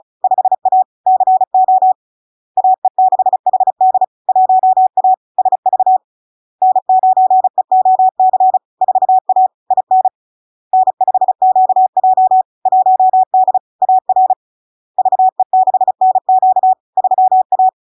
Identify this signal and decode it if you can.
Morse code, heard as a keyed tone, about 36 wpm, 750 Hz. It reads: 5A CO AE6HD 1A SV N9EOC 4A IN N5QJ 1D AR VE6NQ 3A
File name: LARK-FD-2022-Log-in-CW-35wpm-750-Hz